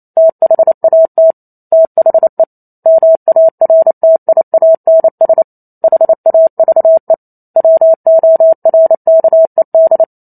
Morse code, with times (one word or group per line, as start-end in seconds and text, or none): 0.2-1.3 THAT
1.7-2.4 THE
2.9-5.4 MARTIANS
5.8-7.1 HAVE
7.6-10.0 WORKED